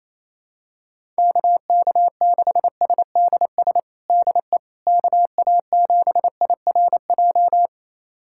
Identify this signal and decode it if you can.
Morse code, heard as a keyed tone, about 28 wpm, 700 Hz. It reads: KK6SDS DE KA7IRJ
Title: USA Callsign DE Callsign - 6 Characters - 28wpm